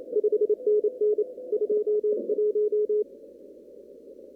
Morse code, heard as a keyed tone, about 27 wpm, 420 Hz. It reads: 5NN 21
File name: CW / kt5j wpx cw-op-k5pi